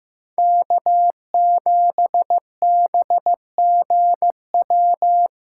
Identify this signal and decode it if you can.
Morse code, heard as a keyed tone, about 18 wpm, 700 Hz. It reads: K7BGW
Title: Speed Racing - USA 1x3 Call Signs 15wpm